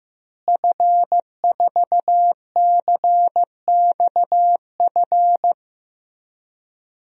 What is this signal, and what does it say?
Morse code, heard as a keyed tone, about 15 wpm, 700 Hz. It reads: F4CXF
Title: DXCC-100 Call Signs - 5 Characters - Repeated 3x - 15wpm